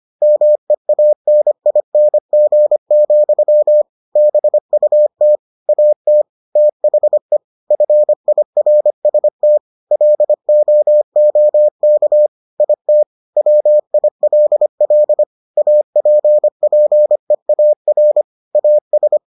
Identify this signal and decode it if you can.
Morse code, heard as a keyed tone, about 25 wpm, 600 Hz. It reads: MEANING, BUT AT THE FIRST LOOK IT WILL APPEAR AS